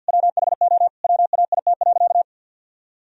Morse code, heard as a keyed tone, about 52 wpm, 700 Hz. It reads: WHO WAIT.
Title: Sentences from Top 1000 Words - Review Part 2 - 50wpm